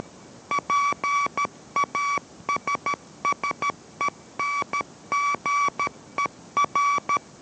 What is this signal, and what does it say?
Morse code, heard as a keyed tone, about 13 wpm, 1140 Hz. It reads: PASSENGER